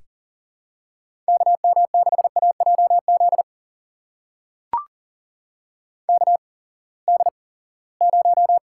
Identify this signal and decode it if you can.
Morse code, heard as a keyed tone, about 38 wpm, spaced as for 22 wpm, 700 Hz. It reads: KM6AJZ KD0
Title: USA Call Signs - 6 Characters - 8x Character Spacing - 40wpm